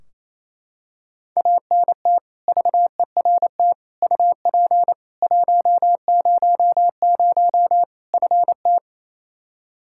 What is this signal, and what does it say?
Morse code, heard as a keyed tone, about 28 wpm, 700 Hz. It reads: ANT VERT UP 100 FT
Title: Practice QSO - Exchange 2 - Info - 28wpm